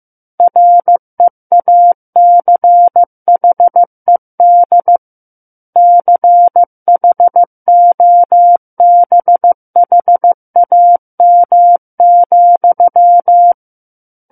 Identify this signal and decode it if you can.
Morse code, heard as a keyed tone, about 15 wpm, 700 Hz. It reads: REACHED CHOBHAM,
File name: Book One / 06-Chapter 6 - 15 WPM